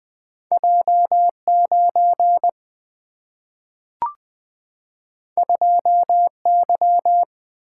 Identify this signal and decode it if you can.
Morse code, heard as a keyed tone, about 20 wpm, 700 Hz. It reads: J9 2Y
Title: Single-Letter Single-Number 20wpm